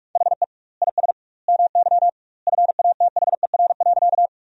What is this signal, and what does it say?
Morse code, heard as a keyed tone, about 46 wpm, 700 Hz. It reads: HE IS MY FATHER.
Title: 2x Word-Spacing - Sentences from Top 300 Words - 45wpm